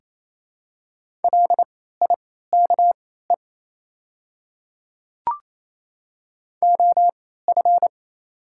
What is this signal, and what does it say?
Morse code, heard as a keyed tone, about 27 wpm, spaced as for 16 wpm, 700 Hz. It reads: LIKE OF
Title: 3x Character Spacing - Top 200 Words - 28wpm